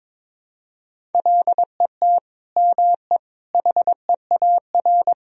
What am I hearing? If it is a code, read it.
Morse code, heard as a keyed tone, about 22 wpm, 700 Hz. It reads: LET ME HEAR
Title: Sentences from Top 1000 Words - Review Part 1 - 22wpm